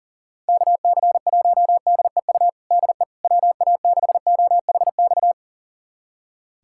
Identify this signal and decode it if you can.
Morse code, heard as a keyed tone, about 40 wpm, 700 Hz. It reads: KC1DEU DE WA6OHX
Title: USA Callsign DE Callsign - 6 Characters - 40wpm